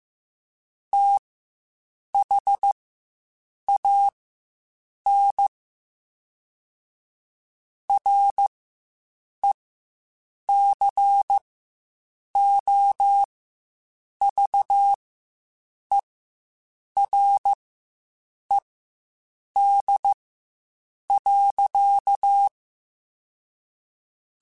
Morse code, heard as a keyed tone, about 15 wpm, spaced as for 7 wpm, 800 Hz. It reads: THAN RECOVERED.
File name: Quotes Spaced / Quotes Spaced-mp3s / Character sp @10